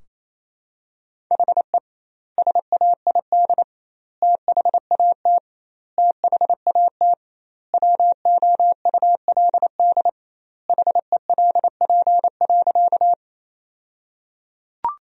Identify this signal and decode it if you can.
Morse code, heard as a keyed tone, about 28 wpm, 700 Hz. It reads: HE SAID THAT THAT WOULD HELP.
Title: 2x Word-Spacing - Sentences from Top 200 Words - 28wpm